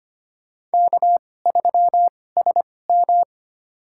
Morse code, heard as a keyed tone, about 25 wpm, spaced as for 18 wpm, 700 Hz. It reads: K3SM
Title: USA Call Signs - 4 Characters - 2x Character Spacing - 25wpm